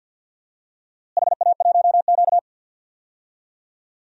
Morse code, heard as a keyed tone, about 50 wpm, 700 Hz. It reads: HA1Q
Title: DXCC-100 Call Signs - 4 Characters - Repeated 3x - 50wpm